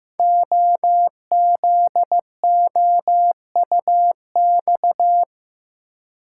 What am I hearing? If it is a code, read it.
Morse code, heard as a keyed tone, about 15 wpm, 700 Hz. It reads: OZOUX
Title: Code Groups - 5 Letter:Numbers - 15wpm